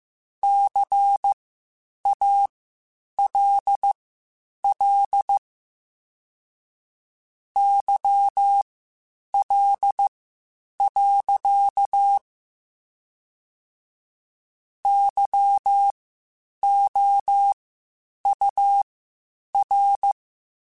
Morse code, heard as a keyed tone, about 15 wpm, spaced as for 8 wpm, 800 Hz. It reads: CALL YL. YOUR
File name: (spaced) / QSOs Spaced-mp3s / WA7QOX 2 @13 sp